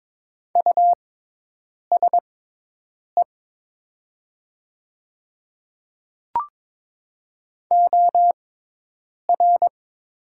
Morse code, heard as a keyed tone, about 22 wpm, spaced as for 8 wpm, 700 Hz. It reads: USE OR
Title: 6x Character Spacing - Top 100 Words - 22wpm